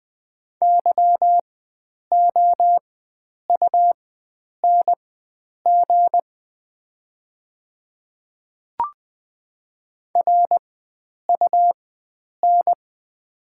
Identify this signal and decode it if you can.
Morse code, heard as a keyed tone, about 20 wpm, spaced as for 9 wpm, 700 Hz. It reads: YOUNG RUN
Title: 4x Character Spacing - Top 500 Words - 20wpm